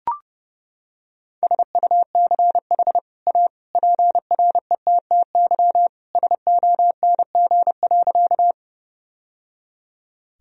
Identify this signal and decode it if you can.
Morse code, heard as a keyed tone, about 32 wpm, 700 Hz. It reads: SUCH A PRETTY SONG.
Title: Sentences from Top 1000 Words - Review Part 2 - 30wpm